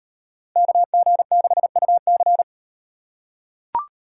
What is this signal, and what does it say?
Morse code, heard as a keyed tone, about 38 wpm, 700 Hz. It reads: KG6UC